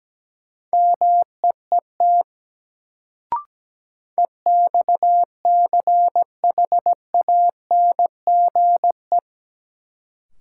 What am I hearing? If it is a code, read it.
Morse code, heard as a keyed tone, about 17 wpm, 700 Hz. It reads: MEET EXCHANGE
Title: Sentences from Top 1000 Words - Review Part 6 - New Word Drill - 17wpm